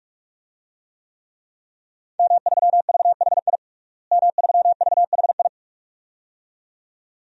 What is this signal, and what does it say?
Morse code, heard as a keyed tone, about 46 wpm, 700 Hz. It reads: M3VHI M3VHI
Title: DXCC-100 Call Signs - 5 Characters - Repeated 2x - 45wpm